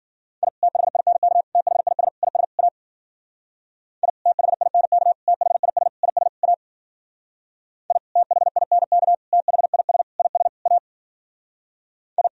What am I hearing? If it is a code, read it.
Morse code, heard as a keyed tone, about 51 wpm, 700 Hz. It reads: I THINK THIS IS A I THINK THIS IS A I THINK THIS IS A I